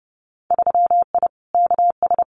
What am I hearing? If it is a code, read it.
Morse code, heard as a keyed tone, about 30 wpm, 700 Hz. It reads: 3I KS
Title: Winter Field Day Exchanges 30wpm